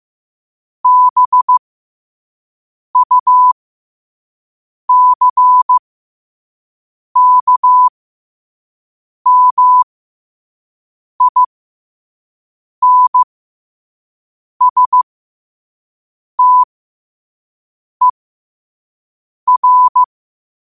Morse code, heard as a keyed tone, about 15 wpm, spaced as for 5 wpm, 1000 Hz. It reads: BUCKMINSTER
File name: qotdfiles / 5wpm